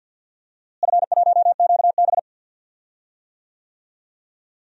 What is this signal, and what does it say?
Morse code, heard as a keyed tone, about 53 wpm, 700 Hz. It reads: F1QB
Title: DXCC-100 Call Signs - 4 Characters - Repeated 2x - 50wpm